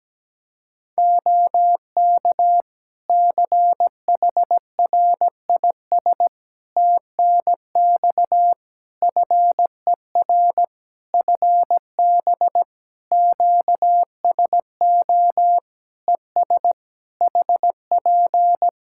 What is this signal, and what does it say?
Morse code, heard as a keyed tone, about 17 wpm, 700 Hz. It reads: OK CHRIS TNX FER FB QSO ES HP
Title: Practice QSO - Exchange 4 - Ending - 17wpm